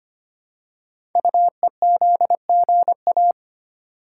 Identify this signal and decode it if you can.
Morse code, heard as a keyed tone, about 25 wpm, 700 Hz. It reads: UEZGA
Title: Code Groups - 5 Letter:Numbers - 25wpm